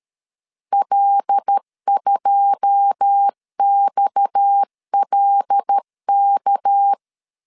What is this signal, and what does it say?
Morse code, heard as a keyed tone, about 13 wpm, 800 Hz. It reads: L2XLK
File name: LEZIONI / 10 SIGNS / 10C DOT 12wpm 2451